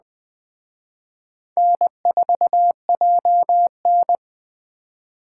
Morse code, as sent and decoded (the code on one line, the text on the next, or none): -. ....- .--- -.
N4JN